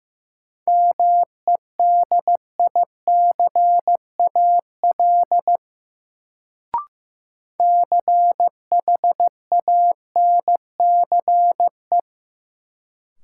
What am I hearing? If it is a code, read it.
Morse code, heard as a keyed tone, about 16 wpm, 700 Hz. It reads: MEDICAL CHANCE
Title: Sentences from Top 800 Words - New Word Drill - 15wpm